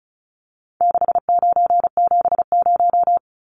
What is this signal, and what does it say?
Morse code, heard as a keyed tone, about 34 wpm, 700 Hz. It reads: 6970
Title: Numbers - 4-Digit 35wpm